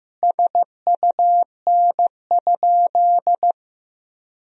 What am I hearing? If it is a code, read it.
Morse code, heard as a keyed tone, about 16 wpm, 700 Hz. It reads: SUN?
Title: Sentences from Top 1000 Words - Review Part 1 - 15wpm